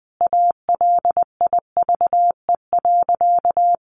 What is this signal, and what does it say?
Morse code, heard as a keyed tone, about 20 wpm, 700 Hz. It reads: ALIVE.
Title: A Sound Of Thunder 20wpm